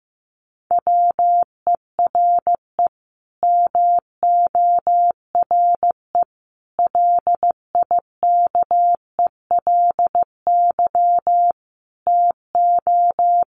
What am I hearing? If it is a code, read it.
Morse code, heard as a keyed tone, about 15 wpm, 700 Hz. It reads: WERE MORE LIKELY TO
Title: Sets of 5 Words Encore2 15wpm